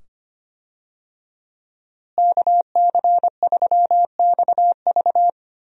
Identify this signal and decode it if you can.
Morse code, heard as a keyed tone, about 25 wpm, 700 Hz. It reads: KC3XV
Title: USA Call Signs - 5 Characters - Repeated 2x - 25wpm